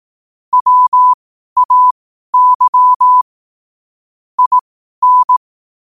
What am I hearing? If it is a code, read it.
Morse code, heard as a keyed tone, about 18 wpm, spaced as for 12 wpm, 1000 Hz. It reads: WAY IN
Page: New quotes every day in morse code at 12 Words per minute.